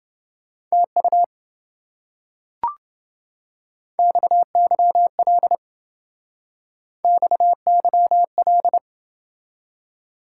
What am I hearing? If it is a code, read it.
Morse code, heard as a keyed tone, about 27 wpm, 700 Hz. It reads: TU XYL XYL